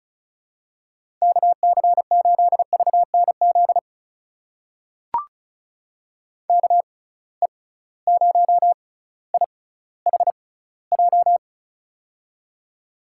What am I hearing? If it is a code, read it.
Morse code, heard as a keyed tone, about 36 wpm, spaced as for 18 wpm, 700 Hz. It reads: KC8VNZ KE0IHJ
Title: USA Call Signs - 6 Characters - 6x Character Spacing - 35wpm